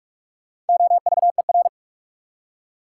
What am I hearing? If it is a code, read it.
Morse code, heard as a keyed tone, about 47 wpm, 700 Hz. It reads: OVER
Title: Prefixes and Suffixes 45wpm